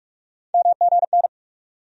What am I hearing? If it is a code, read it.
Morse code, heard as a keyed tone, about 46 wpm, 700 Hz. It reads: MGN
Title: USA Call Sign Postfixes - 3 Letters 45wpm